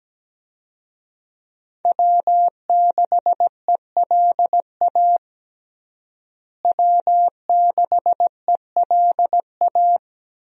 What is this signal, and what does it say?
Morse code, heard as a keyed tone, about 17 wpm, 700 Hz. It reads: W6ELA W6ELA
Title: USA Call Signs - 5 Characters - Repeated 3x - 17wpm